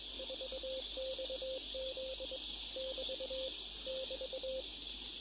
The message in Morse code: 4XZ <BT> <BT>